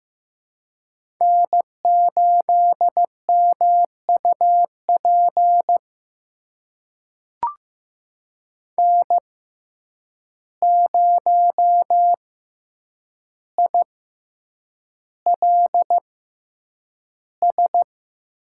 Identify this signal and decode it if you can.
Morse code, heard as a keyed tone, about 15 wpm, spaced as for 8 wpm, 700 Hz. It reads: N8MUP N0ILS
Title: USA Call Signs - 5 Characters - 6x Character Spacing - 15wpm